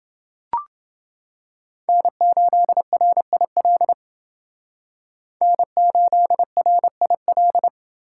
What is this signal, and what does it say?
Morse code, heard as a keyed tone, about 28 wpm, 700 Hz. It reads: N8RIL N8RIL